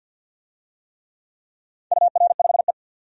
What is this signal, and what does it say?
Morse code, heard as a keyed tone, about 50 wpm, 700 Hz. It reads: UR5E
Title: DXCC-100 Call Signs - 4 Characters - 50wpm